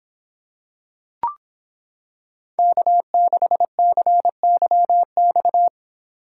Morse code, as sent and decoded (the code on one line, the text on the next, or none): -.- -.... -.-. -.-- -..-
K6CYX